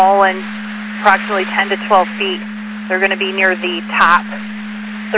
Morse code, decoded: none

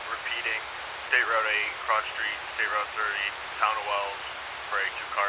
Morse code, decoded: none

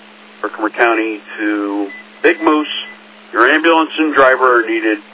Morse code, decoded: none

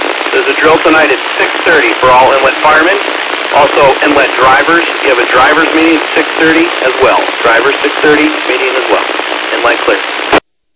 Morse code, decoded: none

